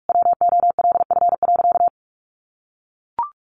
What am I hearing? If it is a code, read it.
Morse code, heard as a keyed tone, about 44 wpm, 700 Hz. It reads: WOLF.
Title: Additional Easy Phrases 45wpm